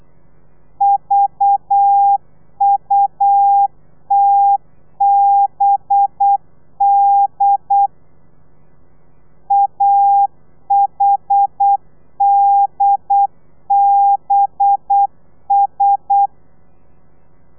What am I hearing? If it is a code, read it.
Morse code, heard as a keyed tone, about 8 wpm, 800 Hz. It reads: VUTBD AHDBS
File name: telegrafia / i2tfj / Lez4 35